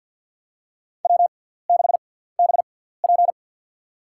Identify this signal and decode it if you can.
Morse code, heard as a keyed tone, about 51 wpm, spaced as for 17 wpm, 690 Hz. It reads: W6BP
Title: USA Call Signs - 4 Characters - 6x Character Spacing - 50wpm